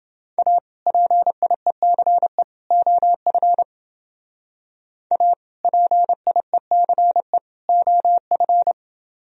Morse code, heard as a keyed tone, about 29 wpm, 700 Hz. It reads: A PIECE OF A PIECE OF